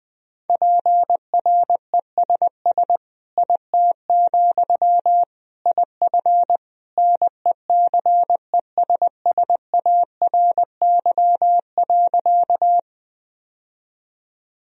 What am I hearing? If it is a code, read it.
Morse code, heard as a keyed tone, about 20 wpm, 700 Hz. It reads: PRESS IT, IF NECESSARY.
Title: Sentences from Top 1000 Words - Review Part 1 - 20wpm